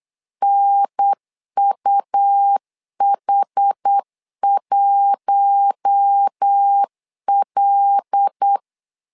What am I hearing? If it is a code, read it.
Morse code, heard as a keyed tone, about 8 wpm, 800 Hz. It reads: NUH1L
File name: LEZIONI / 5 LBVC / 5 L 8wpm 2831